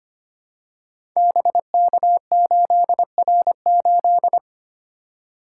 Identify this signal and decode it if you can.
Morse code, heard as a keyed tone, about 26 wpm, 700 Hz. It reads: BK8R8
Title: Code Groups - 5 Letter:Numbers - 25wpm